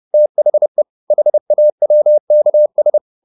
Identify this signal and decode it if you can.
Morse code, heard as a keyed tone, about 30 wpm, 600 Hz. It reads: THE HAWKS
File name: the old man and the sea / the old man and the sea 1 30wpm